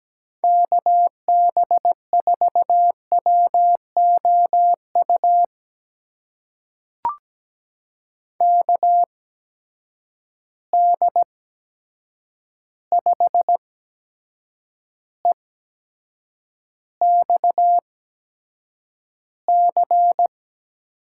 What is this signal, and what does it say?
Morse code, heard as a keyed tone, about 17 wpm, spaced as for 7 wpm, 700 Hz. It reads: KB4WOU KD5EXC
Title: USA Call Signs - 6 Characters - 8x Character Spacing - 17wpm